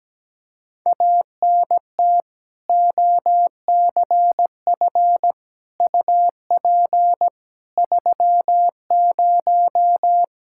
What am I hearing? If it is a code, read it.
Morse code, heard as a keyed tone, about 17 wpm, 700 Hz. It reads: ANT OCF UP 30
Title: Practice QSO - Exchange 2 - Info - 17wpm